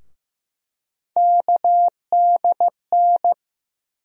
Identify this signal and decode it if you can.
Morse code, heard as a keyed tone, about 15 wpm, 700 Hz. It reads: KDN
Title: USA Call Sign Postfixes - 3 Letters 15wpm